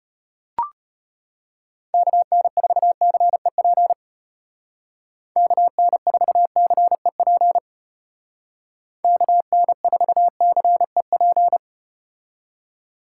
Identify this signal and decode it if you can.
Morse code, heard as a keyed tone, about 34 wpm, 700 Hz. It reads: KN4CEP KN4CEP KN4CEP